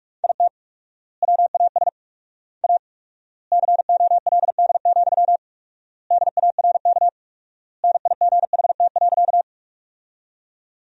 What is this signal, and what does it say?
Morse code, heard as a keyed tone, about 44 wpm, 700 Hz. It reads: IT WAS A COLD, DARK NIGHT.
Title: 4x Word-Spacing - Sentences from Top 500 Words - 45wpm